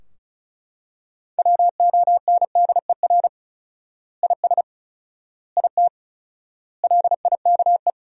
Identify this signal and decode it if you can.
Morse code, heard as a keyed tone, about 36 wpm, 700 Hz. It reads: WONDER IS IT LIKE
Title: 4x Word-Spacing - Sentences from Top 500 Words - 35wpm